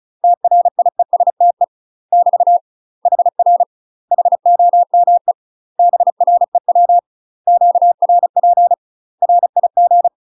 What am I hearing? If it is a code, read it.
Morse code, heard as a keyed tone, about 34 wpm, 700 Hz. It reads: TRIESTE <BT> HR HOME BREW QRP RIG
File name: UK MASTER QSOs TRAINING / 35WPM / UK MASTER QSOs 0021